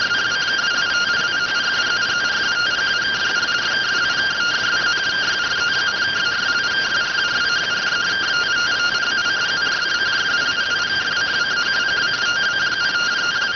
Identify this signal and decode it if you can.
Two-tone teleprinter signal, about 170 Hz shift.